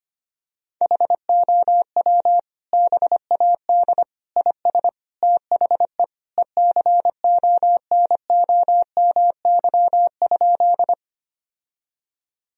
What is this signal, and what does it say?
Morse code, heard as a keyed tone, about 25 wpm, 700 Hz. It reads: HOW BAD IS THE ECONOMY?
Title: Sentences from Top 1000 Words - Review Part 3 - 25wpm